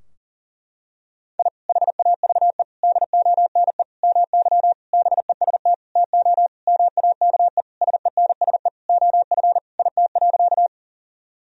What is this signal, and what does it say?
Morse code, heard as a keyed tone, about 40 wpm, 700 Hz. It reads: I HAVE DONE MY BEST TO MAKE SENSE OF IT.
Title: Sentences from Top 1000 Words - Review Part 4 - 40wpm